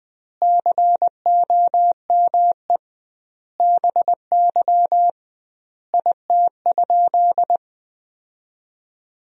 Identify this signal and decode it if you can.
Morse code, heard as a keyed tone, about 20 wpm, 700 Hz. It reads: COME BY IT?
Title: 2x Word-Spacing - Sentences from Top 100 Words - 20wpm